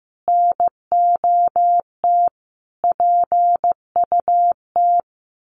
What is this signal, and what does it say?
Morse code, heard as a keyed tone, about 15 wpm, 700 Hz. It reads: NOT PUT
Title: Sentences from Top 1100 Words 15wpm